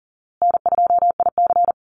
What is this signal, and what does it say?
Morse code, heard as a keyed tone, about 40 wpm, 700 Hz. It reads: N2IC
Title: Easy Call Signs Repeated 2x - 40wpm